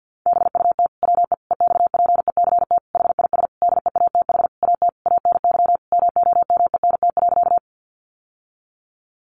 Morse code, heard as a keyed tone, about 50 wpm, 700 Hz. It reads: BUT WE EXPECT HIS DEATH AT ANY MOMENT.
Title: Sentences from Top 1100 Words 50wpm